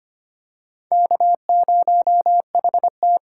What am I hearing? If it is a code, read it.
Morse code, heard as a keyed tone, about 25 wpm, 700 Hz. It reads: K0HT
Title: K1USN SST QSO Practice - 25wpm